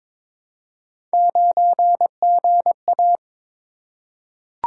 Morse code, heard as a keyed tone, about 22 wpm, 700 Hz. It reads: 9GA
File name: practice / 22wpm / Three Letter-Number Combinations - 22wpm